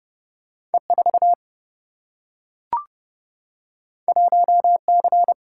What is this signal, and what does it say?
Morse code, heard as a keyed tone, about 30 wpm, 700 Hz. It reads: E4 1C
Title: Single-Letter Single-Number 30wpm